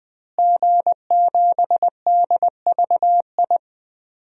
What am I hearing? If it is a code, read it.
Morse code, heard as a keyed tone, about 21 wpm, 700 Hz. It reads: G7DVI
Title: DXCC-100 Call Signs - 5 Characters - Repeated 3x - 20wpm